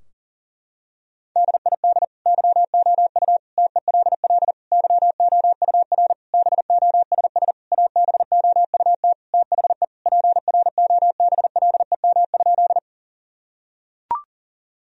Morse code, heard as a keyed tone, about 40 wpm, 700 Hz. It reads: DID YOU TELL YOUR BOSS ABOUT THE PROBLEM?
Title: Sentences from Top 1000 Words - Review Part 4 - 40wpm